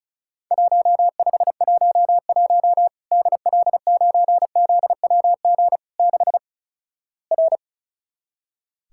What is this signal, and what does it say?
Morse code, heard as a keyed tone, about 36 wpm, 700 Hz. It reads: 1511 DL9ZWG 6 R
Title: Worked All Europe DX Contest - QTC Exchange - 35wpm